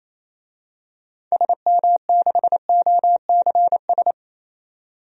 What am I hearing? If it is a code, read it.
Morse code, heard as a keyed tone, about 28 wpm, 700 Hz. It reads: SM6OCS
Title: DXCC-100 Call Signs - 6 Characters - 28wpm